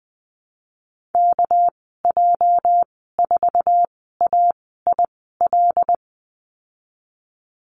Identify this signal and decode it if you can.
Morse code, heard as a keyed tone, about 20 wpm, spaced as for 14 wpm, 700 Hz. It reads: KJ4AIL
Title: 219 Lesson 35g - Number 0 - Call Signs 20wpm